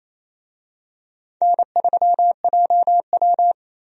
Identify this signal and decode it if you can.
Morse code, heard as a keyed tone, about 28 wpm, 700 Hz. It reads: N3JW
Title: USA Call Signs - 4 Characters - Repeated 3x - 28wpm